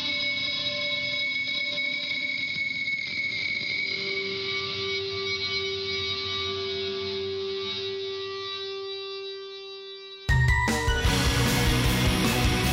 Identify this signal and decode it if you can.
Dial-up modem handshake.